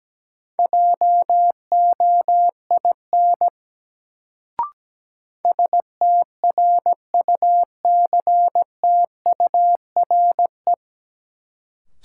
Morse code, heard as a keyed tone, about 17 wpm, 700 Hz. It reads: JOIN STRUCTURE
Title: Sentences from Top 1000 Words - Review Part 6 - New Word Drill - 17wpm